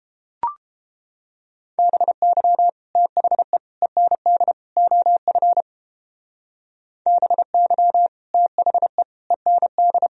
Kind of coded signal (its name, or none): Morse code